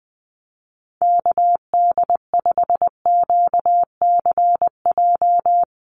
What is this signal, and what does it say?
Morse code, heard as a keyed tone, about 20 wpm, 700 Hz. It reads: KD5QCJ
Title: USA Call Signs Repeated 3x 20wpm